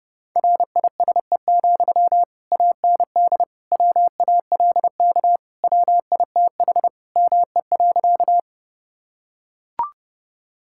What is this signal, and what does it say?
Morse code, heard as a keyed tone, about 32 wpm, 700 Hz. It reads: RISE, AND WALK WITH ME.
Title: Sentences from Top 1000 Words - Review Part 6 - 30wpm